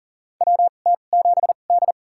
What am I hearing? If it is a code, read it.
Morse code, heard as a keyed tone, about 39 wpm, spaced as for 29 wpm, 700 Hz. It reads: WT7D
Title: USA Call Signs - 4 Characters - 2x Character Spacing - 40wpm